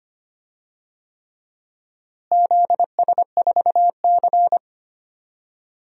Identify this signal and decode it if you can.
Morse code, heard as a keyed tone, about 25 wpm, 700 Hz. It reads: ZS4C
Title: DXCC-100 Call Signs - 4 Characters - Repeated 3x - 25wpm